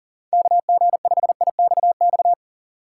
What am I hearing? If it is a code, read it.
Morse code, heard as a keyed tone, about 37 wpm, 700 Hz. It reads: KG5IXX
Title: USA Call Signs - 2x Character Spacing - 40wpm